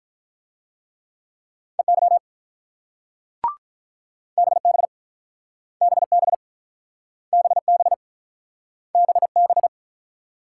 Morse code, heard as a keyed tone, about 46 wpm, 700 Hz. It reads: EY BB BB BB BB